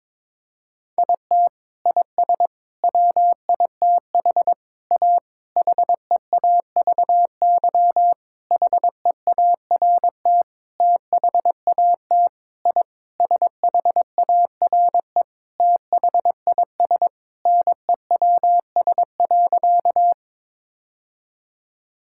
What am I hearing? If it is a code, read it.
Morse code, heard as a keyed tone, about 22 wpm, 700 Hz. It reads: IT IS WITH A HEAVY HEART THAT I SHARE THIS NEWS.
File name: practice / 22wpm / Sentences with 8-Word Phrases - 22wpm